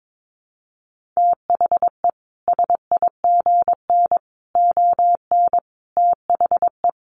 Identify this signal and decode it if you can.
Morse code, heard as a keyed tone, about 22 wpm, 700 Hz. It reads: THE SIGN ON THE
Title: A Sound Of Thunder 22wpm